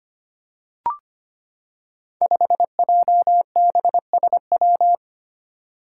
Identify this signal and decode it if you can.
Morse code, heard as a keyed tone, about 25 wpm, 700 Hz. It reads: 5JBSW
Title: Five Challenging Letter-Number Combinations 25wpm